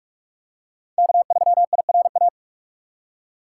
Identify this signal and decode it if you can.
Morse code, heard as a keyed tone, about 46 wpm, 700 Hz. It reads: K3IRA
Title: USA Call Signs - 5 Characters - 4x Character Spacing - 45wpm